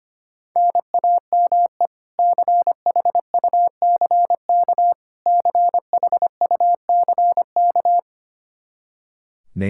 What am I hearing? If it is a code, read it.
Morse code, heard as a keyed tone, about 25 wpm, 700 Hz. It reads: NAME CHUCK CHUCK
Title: K1USN SST QSO Practice 25wpm